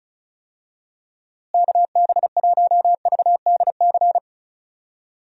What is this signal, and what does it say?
Morse code, heard as a keyed tone, about 34 wpm, 700 Hz. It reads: KB1VDC
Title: USA Call Signs - 6 Characters - Repeated 3x - 35wpm